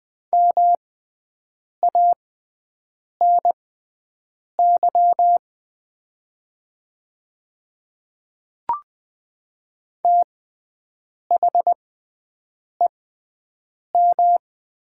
Morse code, heard as a keyed tone, about 20 wpm, spaced as for 7 wpm, 700 Hz. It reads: MANY THEM
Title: 6x Character Spacing - Top 200 Words - 20wpm